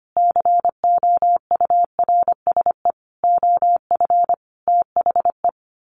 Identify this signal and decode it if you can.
Morse code, heard as a keyed tone, about 25 wpm, 700 Hz. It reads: COURSE OF THE
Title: Sets of 5 Words Encore 25wpm